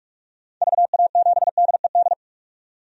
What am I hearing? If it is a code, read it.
Morse code, heard as a keyed tone, about 46 wpm, 700 Hz. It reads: VA7DED
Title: DXCC-100 Call Signs - 6 Characters - 45wpm